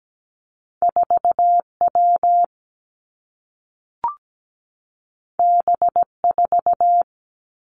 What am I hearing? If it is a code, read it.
Morse code, heard as a keyed tone, about 17 wpm, 700 Hz. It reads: 4 W B 4
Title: Two Challenging Letter-Number Combinations 17wpm